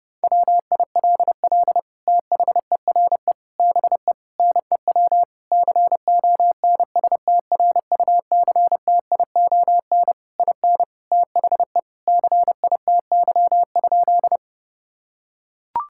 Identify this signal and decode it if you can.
Morse code, heard as a keyed tone, about 30 wpm, 700 Hz. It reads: WILL THERE BE NEW CONSTRUCTION IN THE CITY?
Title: Sentences from Top 1600 Words 30wpm